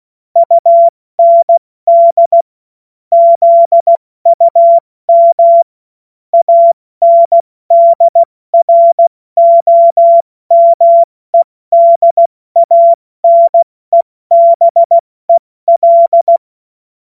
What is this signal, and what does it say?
Morse code, heard as a keyed tone, about 16 wpm, 680 Hz. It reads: UND ZUM ANDROMEDANEBEL